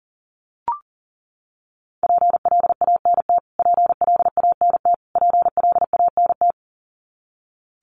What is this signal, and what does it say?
Morse code, heard as a keyed tone, about 40 wpm, 700 Hz. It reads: PLANT PLANT PLANT
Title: Top 500 Words Repeated 40wpm